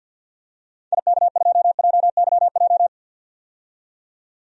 Morse code, heard as a keyed tone, about 48 wpm, 700 Hz. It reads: IK2JYJ
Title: DXCC-100 Call Signs - 6 Characters - 50wpm